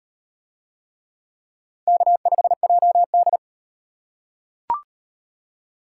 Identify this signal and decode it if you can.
Morse code, heard as a keyed tone, about 34 wpm, 700 Hz. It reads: K5JD